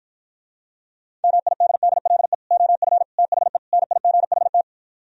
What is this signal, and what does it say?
Morse code, heard as a keyed tone, about 52 wpm, 700 Hz. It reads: MIDDLE OF THE NIGHT